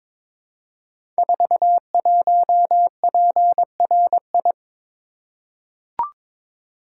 Morse code, heard as a keyed tone, about 22 wpm, 700 Hz. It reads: 41PRI
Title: Code Groups - 5 Letter:Numbers - 22wpm